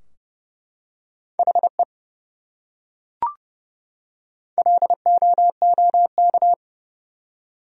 Morse code, heard as a keyed tone, about 30 wpm, 700 Hz. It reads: HE LOOK
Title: Top 100 Words 30wpm